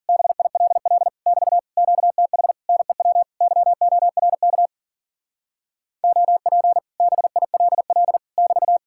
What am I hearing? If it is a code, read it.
Morse code, heard as a keyed tone, about 51 wpm, 700 Hz. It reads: BILL <BT> QTH NEW YORK OP BILL <BT>